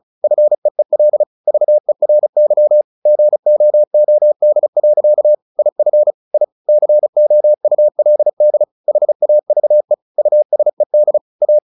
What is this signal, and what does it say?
Morse code, heard as a keyed tone, about 34 wpm, 600 Hz. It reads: FEEL VERY GOOD. IF I COULD HAVE USED A
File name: the old man and the sea 2 35wpm